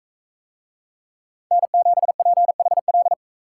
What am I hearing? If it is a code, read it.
Morse code, heard as a keyed tone, about 42 wpm, 700 Hz. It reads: N7PHL